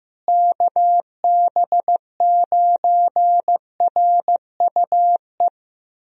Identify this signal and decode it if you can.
Morse code, heard as a keyed tone, about 15 wpm, 700 Hz. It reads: KB9RUE
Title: USA Call Signs - 6 Characters - Repeated 3x - 15wpm